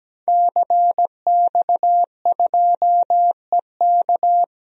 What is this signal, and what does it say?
Morse code, heard as a keyed tone, about 17 wpm, 700 Hz. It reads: CX2EK
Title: DXCC-100 Call Signs - 5 Characters - 17wpm